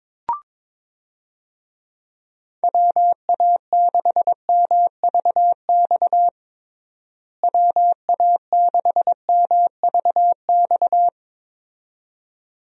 Morse code, heard as a keyed tone, about 21 wpm, 700 Hz. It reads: WA6MVX WA6MVX
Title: USA Call Signs - 6 Characters - Repeated 2x - 22wpm